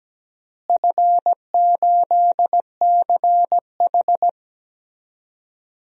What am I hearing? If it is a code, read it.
Morse code, heard as a keyed tone, about 17 wpm, 700 Hz. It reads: F8CH
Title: DXCC-100 Call Signs - 4 Characters - Repeated 3x - 17wpm